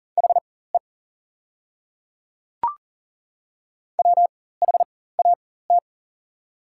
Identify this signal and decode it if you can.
Morse code, heard as a keyed tone, about 40 wpm, spaced as for 19 wpm, 700 Hz. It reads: HE WHAT
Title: 4x Character Spacing - Top 200 Words - 40wpm